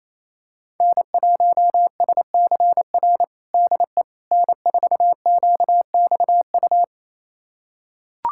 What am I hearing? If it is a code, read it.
Morse code, heard as a keyed tone, about 28 wpm, 700 Hz. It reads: N1SCR DE N4QXU
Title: USA Callsign DE Callsign - 5 Characters - 28wpm